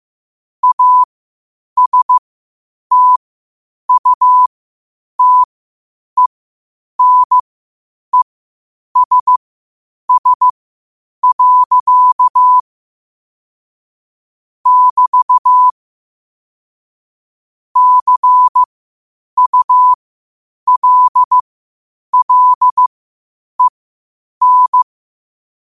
Morse code, heard as a keyed tone, about 15 wpm, spaced as for 8 wpm, 1000 Hz. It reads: ASTUTENESS. <BT> CULLEN
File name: qotdfiles / 8wpm